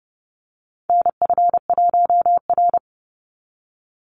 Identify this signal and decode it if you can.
Morse code, heard as a keyed tone, about 30 wpm, 700 Hz. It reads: NF1R
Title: NA SOTA Chaser Callsigns 30wpm